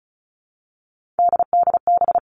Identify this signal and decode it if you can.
Morse code, heard as a keyed tone, about 34 wpm, 700 Hz. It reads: DDB
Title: All Cut Numbers - 3-Digit 35wpm